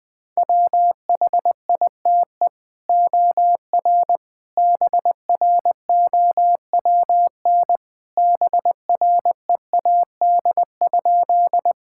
Morse code, heard as a keyed tone, about 21 wpm, 700 Hz. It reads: WHITE OR BROWN BREAD?
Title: Sentences from Top 1600 Words 20wpm